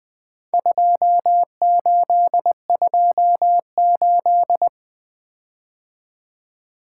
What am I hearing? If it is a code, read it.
Morse code, heard as a keyed tone, about 22 wpm, 700 Hz. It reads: 2828
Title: Speed-Racing - Four Letter-Number Combinations - 20wpm